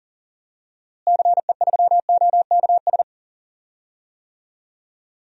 Morse code, heard as a keyed tone, about 37 wpm, 700 Hz. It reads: CE3OKS
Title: DXCC-100 Call Signs - 6 Characters - Repeated 2x - 40wpm